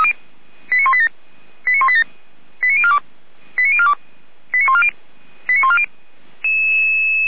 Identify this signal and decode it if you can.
Five-tone selective call (ZVEI), heard as quick runs of five tones, tones of about 70 ms. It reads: 89187 89187 89042 89042 89140 89140
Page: ZVEI